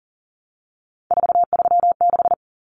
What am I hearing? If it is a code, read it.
Morse code, heard as a keyed tone, about 40 wpm, 700 Hz. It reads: 436
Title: Common Cut Numbers - 3-Digit 40wpm